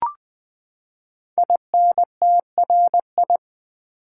Morse code, heard as a keyed tone, about 20 wpm, 700 Hz. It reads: INTRI
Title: Build A Word - 5 Letters At A Time - 10-Letter Words - 20wpm